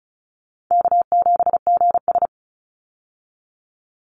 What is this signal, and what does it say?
Morse code, heard as a keyed tone, about 34 wpm, 700 Hz. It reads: K7GS
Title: Call Signs Repeated 3x - 35wpm